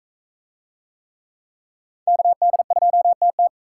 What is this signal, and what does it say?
Morse code, heard as a keyed tone, about 40 wpm, 700 Hz. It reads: KD2TT